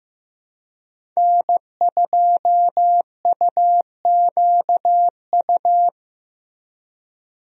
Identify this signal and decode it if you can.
Morse code, heard as a keyed tone, about 15 wpm, 700 Hz. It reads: N2UQU
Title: USA Call Signs - 5 Characters - Repeated 2x - 15wpm